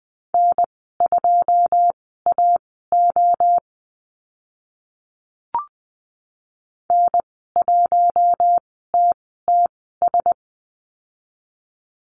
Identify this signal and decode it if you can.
Morse code, heard as a keyed tone, about 20 wpm, spaced as for 14 wpm, 700 Hz. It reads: N2AO N1TTS
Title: 082 Lesson 14g - Number 2 - Call Signs 20wpm